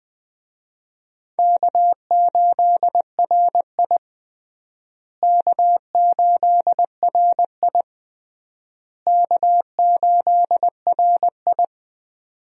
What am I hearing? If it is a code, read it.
Morse code, heard as a keyed tone, about 20 wpm, 700 Hz. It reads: K8RI K8RI K8RI
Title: USA Call Signs - 4 Characters - Repeated 3x - 20wpm